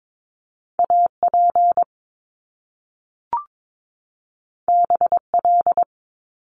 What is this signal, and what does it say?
Morse code, heard as a keyed tone, about 22 wpm, 700 Hz. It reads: AP BL
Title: Top Two Letter Combinations in Words 22wpm